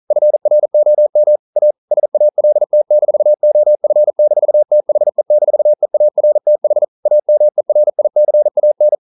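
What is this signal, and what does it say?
Morse code, heard as a keyed tone, about 42 wpm, 600 Hz. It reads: FROM A SALT-OF-THE-EARTH AMERICAN
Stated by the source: Morse code practice at 41 wpm, from CW med Gnister.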